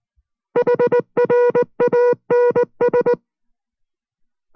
Morse code, heard as a keyed tone, about 19 wpm, 490 Hz. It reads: HRANS